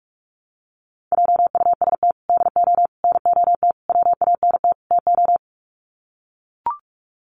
Morse code, heard as a keyed tone, about 44 wpm, 700 Hz. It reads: JUST DO NOT WANT TO
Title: Sets of 5 Words Encore 45wpm